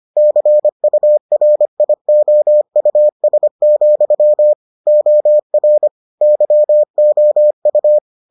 Morse code, heard as a keyed tone, about 25 wpm, 600 Hz. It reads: CURIOUS, OR YOU